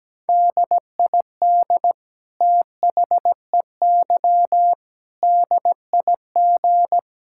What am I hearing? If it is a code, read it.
Morse code, heard as a keyed tone, about 17 wpm, 700 Hz. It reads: DID THEY DIG
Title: Sentences from Top 1600 Words 17wpm